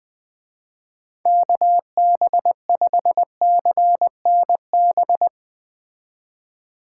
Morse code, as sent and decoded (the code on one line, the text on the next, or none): -.- -... ..... -.-. -. -...
KB5CNB